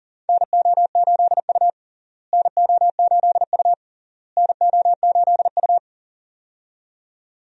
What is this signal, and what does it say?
Morse code, heard as a keyed tone, about 40 wpm, 700 Hz. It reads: NO8U NO8U NO8U
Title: USA Call Signs - 4 Characters - Repeated 3x - 40wpm